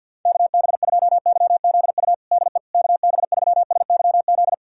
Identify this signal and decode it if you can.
Morse code, heard as a keyed tone, about 52 wpm, 700 Hz. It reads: KB2YZU DE KB3SYZ
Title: USA Callsign DE Callsign 50wpm